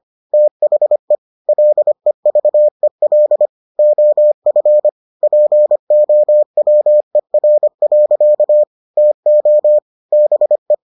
Morse code, heard as a keyed tone, about 25 wpm, 600 Hz. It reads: THE LEVEL OF POWER. TO BE